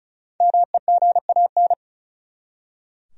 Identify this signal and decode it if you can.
Morse code, heard as a keyed tone, about 36 wpm, 700 Hz. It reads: MEGAN
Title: USA Common Names - 35wpm